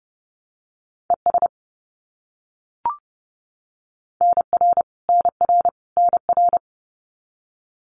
Morse code, heard as a keyed tone, about 30 wpm, 700 Hz. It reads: ES NR NR NR
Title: Abbreviations Repeated 3x - 30wpm